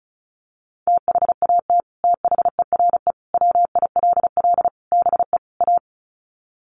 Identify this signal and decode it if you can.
Morse code, heard as a keyed tone, about 34 wpm, 700 Hz. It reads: THAT THERE WILL BE A
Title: Sets of 5 Words Encore2 35wpm